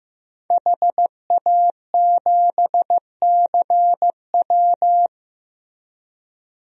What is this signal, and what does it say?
Morse code, heard as a keyed tone, about 15 wpm, 700 Hz. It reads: HA7CW
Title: DXCC-100 Call Signs - 5 Characters - Repeated 3x - 15wpm